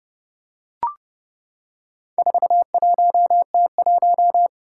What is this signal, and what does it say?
Morse code, heard as a keyed tone, about 30 wpm, 700 Hz. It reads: 41T1
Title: Four Challenging Letter-Number Combinations 30wpm